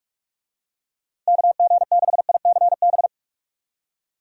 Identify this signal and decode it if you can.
Morse code, heard as a keyed tone, about 47 wpm, 700 Hz. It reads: KG6ICB
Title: USA Call Signs - 6 Characters - 45wpm